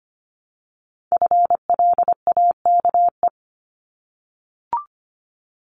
Morse code, heard as a keyed tone, about 25 wpm, 700 Hz. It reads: FLAKE
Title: Multi-letter Phonemes 25wpm